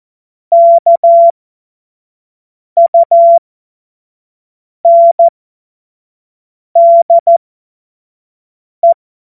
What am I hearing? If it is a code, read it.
Morse code, heard as a keyed tone, about 14 wpm, spaced as for 5 wpm, 680 Hz. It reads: KUNDE
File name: Morsekurs / Witze / Witz 25. April 2023 25 F70